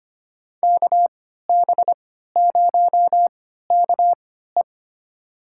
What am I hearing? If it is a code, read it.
Morse code, heard as a keyed tone, about 25 wpm, spaced as for 14 wpm, 700 Hz. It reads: KB0KE
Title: USA Call Signs - 5 Characters - 3x Character Spacing - 25wpm